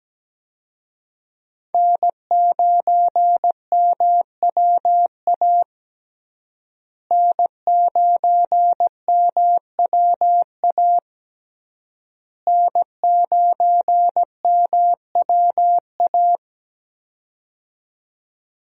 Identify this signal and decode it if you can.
Morse code, heard as a keyed tone, about 17 wpm, 700 Hz. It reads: N9MWA N9MWA N9MWA
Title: USA Call Signs - 5 Characters - Repeated 3x - 17wpm